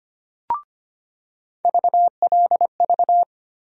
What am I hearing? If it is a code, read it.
Morse code, heard as a keyed tone, about 25 wpm, 700 Hz. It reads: VLV
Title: USA Call Sign Postfixes - 3 Letters 25wpm